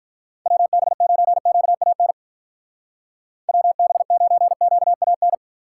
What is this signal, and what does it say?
Morse code, heard as a keyed tone, about 45 wpm, 700 Hz. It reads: WB9QAN WB9QAN